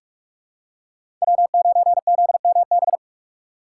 Morse code, heard as a keyed tone, about 47 wpm, 700 Hz. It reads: W9ZMB
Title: USA Call Signs - 5 Characters - 6x Character Spacing - 45wpm